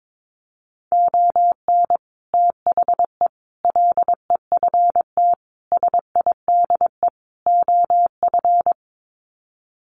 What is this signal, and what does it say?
Morse code, heard as a keyed tone, about 22 wpm, 700 Hz. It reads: ON THE LEFT SIDE OF
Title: Sets of 5 Words Encore2 22wpm